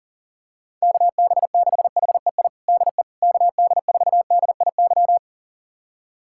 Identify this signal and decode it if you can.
Morse code, heard as a keyed tone, about 37 wpm, 700 Hz. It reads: KB6HEI DE KD4DIY
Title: USA Callsign DE Callsign - 6 Characters - 40wpm